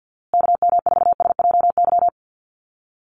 Morse code, heard as a keyed tone, about 50 wpm, 700 Hz. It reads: KM4SJY
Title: USA Call Signs Repeated 2x 50wpm